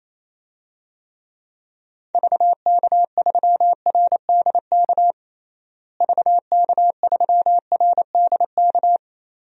Morse code, heard as a keyed tone, about 28 wpm, 700 Hz. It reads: VK3RDK VK3RDK
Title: DXCC-100 Call Signs - 6 Characters - Repeated 2x - 28wpm